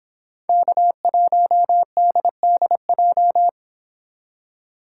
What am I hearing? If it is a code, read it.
Morse code, heard as a keyed tone, about 29 wpm, 700 Hz. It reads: K1DDJ